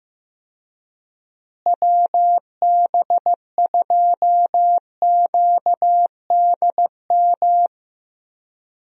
Morse code, heard as a keyed tone, about 15 wpm, 700 Hz. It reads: WB2QDM
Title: USA Call Signs - 6 Characters - Repeated 3x - 15wpm